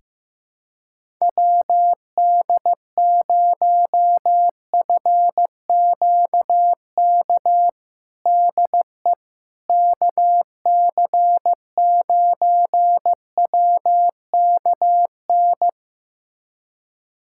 Morse code, heard as a keyed tone, about 15 wpm, 700 Hz. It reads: WD0FQK DE KC9WKN
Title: USA Callsign DE Callsign 15wpm